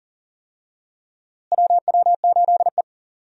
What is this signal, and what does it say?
Morse code, heard as a keyed tone, about 40 wpm, 700 Hz. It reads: WW8E
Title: USA Call Signs - 4 Characters - Repeated 3x - 40wpm